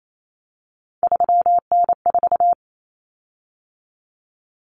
Morse code, heard as a keyed tone, about 28 wpm, 700 Hz. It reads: 3N4
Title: Common Cut Numbers - 3-Digit 28wpm